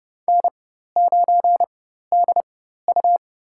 Morse code, heard as a keyed tone, about 29 wpm, spaced as for 14 wpm, 700 Hz. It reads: N9DU
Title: USA Call Signs - 4x Character Spacing - 30wpm